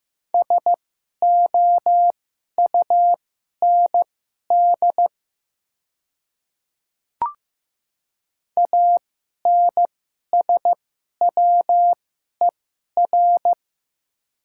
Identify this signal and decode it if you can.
Morse code, heard as a keyed tone, about 15 wpm, spaced as for 11 wpm, 700 Hz. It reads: SOUND ANSWER
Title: 2x Character Spacing - Top 500 Words - 15wpm